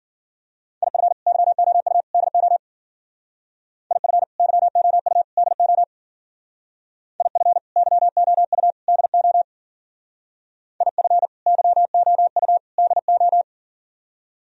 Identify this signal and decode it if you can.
Morse code, heard as a keyed tone, about 52 wpm, 700 Hz. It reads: IF YOU DO IF YOU DO IF YOU DO IF YOU DO